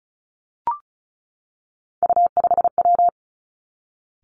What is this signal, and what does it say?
Morse code, heard as a keyed tone, about 36 wpm, 700 Hz. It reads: U5W
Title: Three Challenging Letter-Number Combinations 35wpm